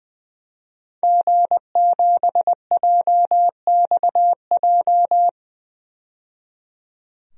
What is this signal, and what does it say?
Morse code, heard as a keyed tone, about 20 wpm, 700 Hz. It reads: G7JXJ
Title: DXCC-100 Call Signs - 5 Characters - 20wpm